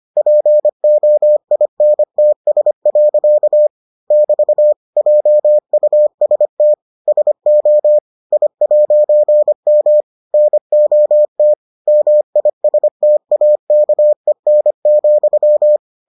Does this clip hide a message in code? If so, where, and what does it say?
Yes: Morse code from 0.2 to 15.8 s, heard as a keyed tone, about 25 wpm, 600 Hz. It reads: POINTS. <BT> JUST SO I'M NOT MISTAKEN,